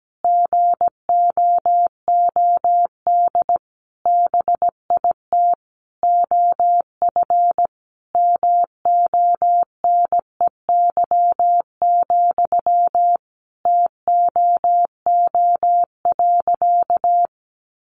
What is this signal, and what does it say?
Morse code, heard as a keyed tone, about 17 wpm, 700 Hz. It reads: GOOD BIT OF MONEY, TOO.
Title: Sentences from Top 900 Words 17wpm